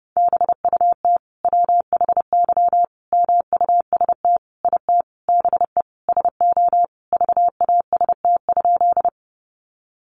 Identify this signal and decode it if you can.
Morse code, heard as a keyed tone, about 30 wpm, 700 Hz. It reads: BUT WHY MUST IT BE SO VAST?
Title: Sentences from Top 1300 Words 30wpm